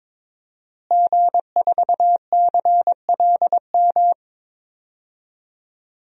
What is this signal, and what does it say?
Morse code, heard as a keyed tone, about 22 wpm, 700 Hz. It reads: G4CLM
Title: DXCC-100 Call Signs - 5 Characters - Repeated 3x - 22wpm